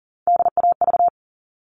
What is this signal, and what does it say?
Morse code, heard as a keyed tone, about 40 wpm, 700 Hz. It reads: DAV
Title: All Cut Numbers - 3-Digit 40wpm